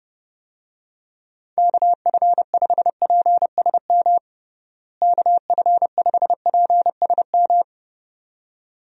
Morse code, heard as a keyed tone, about 30 wpm, 700 Hz. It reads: KF5PSM KF5PSM
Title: USA Call Signs - 6 Characters - Repeated 2x - 30wpm